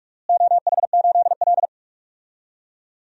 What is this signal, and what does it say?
Morse code, heard as a keyed tone, about 46 wpm, 700 Hz. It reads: OH8L
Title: DXCC-100 Call Signs - 4 Characters - Repeated 2x - 45wpm